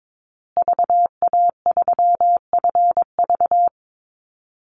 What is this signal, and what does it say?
Morse code, heard as a keyed tone, about 22 wpm, 700 Hz. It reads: VA3FV
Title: Call Signs Repeated 2x - 22wpm